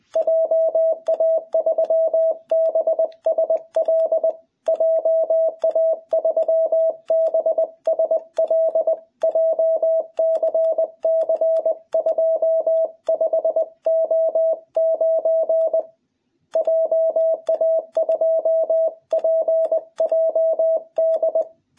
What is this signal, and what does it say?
Morse code, heard as a keyed tone, about 20 wpm, 660 Hz. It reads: JA3BSL JA3BSL JCC 25O9 JA2PJD
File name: jh8cbh / 2021cw